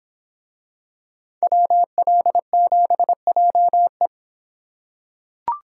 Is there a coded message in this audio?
Yes: Morse code from 1.4 to 4.1 s, heard as a keyed tone, about 26 wpm, 700 Hz. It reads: WL7JE